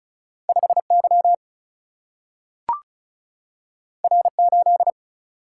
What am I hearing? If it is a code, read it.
Morse code, heard as a keyed tone, about 36 wpm, 700 Hz. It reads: 5Y R8
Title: Single-Letter Single-Number 35wpm